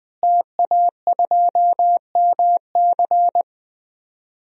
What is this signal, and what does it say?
Morse code, heard as a keyed tone, about 20 wpm, 700 Hz. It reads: TA2MC
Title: DXCC-100 Call Signs - 5 Characters - Repeated 3x - 20wpm